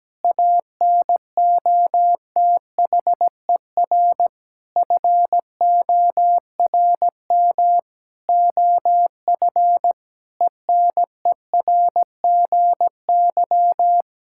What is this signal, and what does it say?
Morse code, heard as a keyed tone, about 17 wpm, 700 Hz. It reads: ANOTHER FORM OF ENERGY
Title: Sentences from Top 1000 Words - Review Part 6 - 17wpm